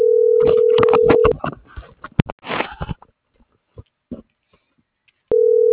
Ringback tone (North America), 440 + 480 Hz.